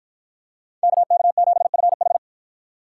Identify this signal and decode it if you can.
Morse code, heard as a keyed tone, about 53 wpm, 700 Hz. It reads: KK7FH